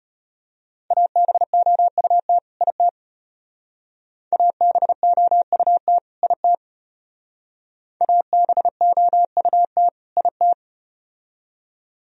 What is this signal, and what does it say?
Morse code, heard as a keyed tone, about 33 wpm, 700 Hz. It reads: ABOUT IT ABOUT IT ABOUT IT